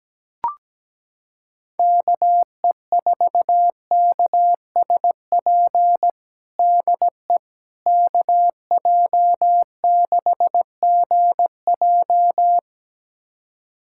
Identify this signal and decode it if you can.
Morse code, heard as a keyed tone, about 17 wpm, 700 Hz. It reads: KE4KSP DE KJ6GJ
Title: USA Callsign DE Callsign 17wpm